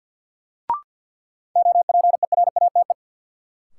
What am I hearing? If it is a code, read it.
Morse code, heard as a keyed tone, about 52 wpm, 700 Hz. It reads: OPERATE
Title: Sentences from Top 1500 Words - New Word Drill - 50wpm